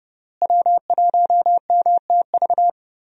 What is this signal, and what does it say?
Morse code, heard as a keyed tone, about 30 wpm, 700 Hz. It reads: W1MTV
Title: USA Call Signs - 5 Characters - Repeated 3x - 30wpm